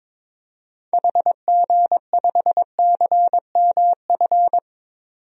Morse code, heard as a keyed tone, about 22 wpm, 700 Hz. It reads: HG5CMF
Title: DXCC-100 Call Signs - 6 Characters - 22wpm